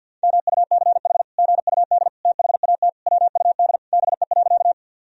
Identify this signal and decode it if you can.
Morse code, heard as a keyed tone, about 50 wpm, 700 Hz. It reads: MUCH GUD THAT WUD BE.
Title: Sentences from Top 200 Words and Abbreviations 50wpm